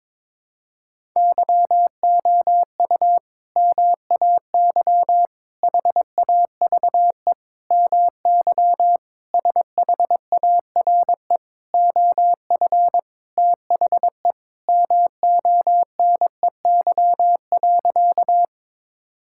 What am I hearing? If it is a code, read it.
Morse code, heard as a keyed tone, about 22 wpm, 700 Hz. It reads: YOU MAY HAVE MY SHARE OF THE MONEY.
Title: Sentences from Top 1000 Words - Review Part 6 - 22wpm